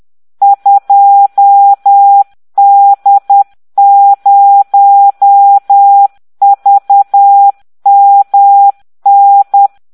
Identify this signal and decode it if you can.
Morse code, heard as a keyed tone, about 10 wpm, 800 Hz. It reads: 2D0VMN